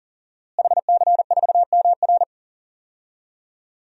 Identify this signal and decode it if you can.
Morse code, heard as a keyed tone, about 38 wpm, 690 Hz. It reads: HC4MR
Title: DXCC-100 Call Signs - 5 Characters - Repeated 2x - 40wpm